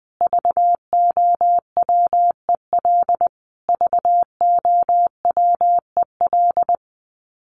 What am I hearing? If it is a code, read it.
Morse code, heard as a keyed tone, about 20 wpm, 700 Hz. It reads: VOWEL VOWEL
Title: Top 500 Words Repeated 20wpm